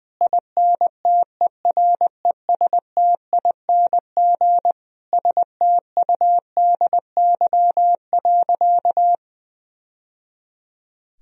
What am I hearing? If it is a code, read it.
Morse code, heard as a keyed tone, about 20 wpm, 700 Hz. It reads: INTERESTING STUDY.
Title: Sentences from Top 1000 Words - Review Part 1 - 20wpm